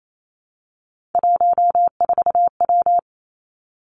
Morse code, heard as a keyed tone, about 28 wpm, 700 Hz. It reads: 14W
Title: Three Challenging Letter-Number Combinations 28wpm